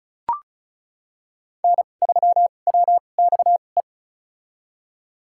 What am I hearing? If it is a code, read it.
Morse code, heard as a keyed tone, about 34 wpm, spaced as for 25 wpm, 700 Hz. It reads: N3WXE
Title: USA Call Signs - 5 Characters - 2x Character Spacing - 35wpm